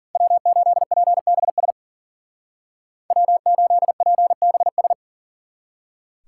Morse code, heard as a keyed tone, about 50 wpm, 700 Hz. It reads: W8PBS W8PBS
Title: Speed Racing - USA 1x3 Call Signs 40wpm